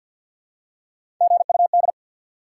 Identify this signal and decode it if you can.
Morse code, heard as a keyed tone, about 49 wpm, 700 Hz. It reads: GUD
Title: Top 180 QSO Elements 50wpm